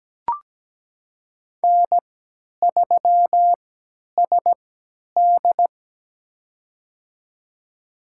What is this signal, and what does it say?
Morse code, heard as a keyed tone, about 17 wpm, spaced as for 10 wpm, 700 Hz. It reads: N3SD
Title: USA Call Signs - 4 Characters - 3x Character Spacing - 17wpm